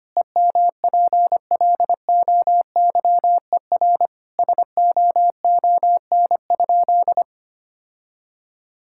Morse code, heard as a keyed tone, about 25 wpm, 700 Hz. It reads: EMPLOYER SOON?
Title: Sentences from Top 1600 Words 25wpm